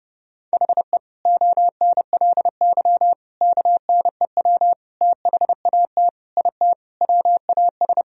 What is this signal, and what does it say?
Morse code, heard as a keyed tone, about 30 wpm, 700 Hz. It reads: HE ONLY KNEW THAT IT WAS
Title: Sentences from Top 1000 Words - Review Part 2 - 30wpm